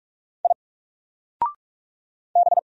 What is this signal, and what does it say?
Morse code, heard as a keyed tone, about 46 wpm, 700 Hz. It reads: I B
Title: Single Letter-Number 45wpm